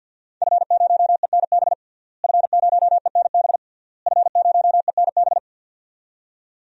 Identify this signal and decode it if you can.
Morse code, heard as a keyed tone, about 52 wpm, 700 Hz. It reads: F0ENB F0ENB F0ENB
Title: DXCC-100 Call Signs - 5 Characters - Repeated 3x - 50wpm